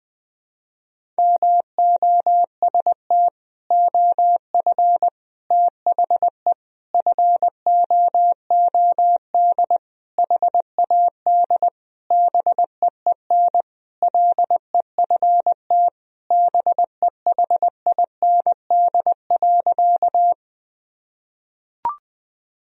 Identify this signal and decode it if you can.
Morse code, heard as a keyed tone, about 20 wpm, 700 Hz. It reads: MOST OF THE FOOD HAD BEEN LEFT BEHIND.
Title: Sentences from Top 1000 Words - Review Part 1 - 20wpm